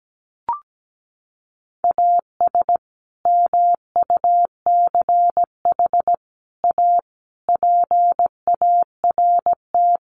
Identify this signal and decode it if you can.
Morse code, heard as a keyed tone, about 17 wpm, 700 Hz. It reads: AS MUCH A PART
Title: Sets of 5 Words Encore2 17wpm